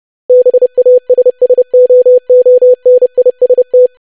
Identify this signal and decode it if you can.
Morse code, heard as a keyed tone, about 30 wpm, 500 Hz. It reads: BASSOONIST
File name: lesson / web / longwords-25wpm